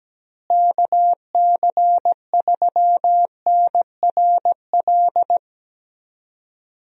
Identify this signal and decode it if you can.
Morse code, heard as a keyed tone, about 17 wpm, 700 Hz. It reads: KC3NRL
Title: USA Call Signs - 6 Characters - Repeated 3x - 17wpm